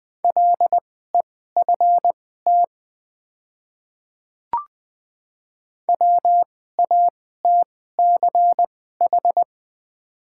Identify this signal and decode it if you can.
Morse code, heard as a keyed tone, about 20 wpm, spaced as for 14 wpm, 700 Hz. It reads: LEFT WATCH
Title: 2x Character Spacing - Top 300 Words - 20wpm